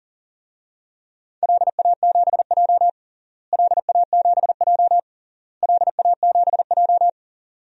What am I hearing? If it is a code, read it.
Morse code, heard as a keyed tone, about 37 wpm, 700 Hz. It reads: LA7J LA7J LA7J
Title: DXCC-100 Call Signs - 4 Characters - Repeated 3x - 40wpm